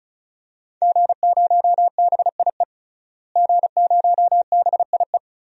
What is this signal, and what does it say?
Morse code, heard as a keyed tone, about 34 wpm, 700 Hz. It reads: G0BIE G0BIE
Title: DXCC-100 Call Signs - 5 Characters - Repeated 3x - 35wpm